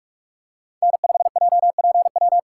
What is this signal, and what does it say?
Morse code, heard as a keyed tone, about 46 wpm, 700 Hz. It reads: N5JPW
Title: USA Call Signs - 3x Character Spacing - 45wpm